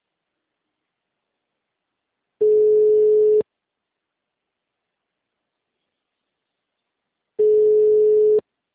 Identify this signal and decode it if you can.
Ringback tone (Europe, 425 Hz).
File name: out- -2011- - - .45324